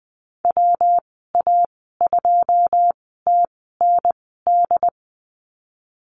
Morse code, heard as a keyed tone, about 20 wpm, spaced as for 15 wpm, 700 Hz. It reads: WA2TND
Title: 167 Lesson 27g - Number 7 - Call Signs 20wpm